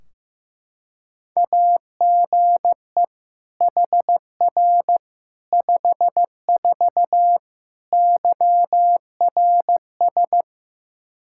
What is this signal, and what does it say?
Morse code, heard as a keyed tone, about 15 wpm, 700 Hz. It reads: AGE HR 54 YRS
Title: Practice QSO - Exchange 3 - More Info - 15wpm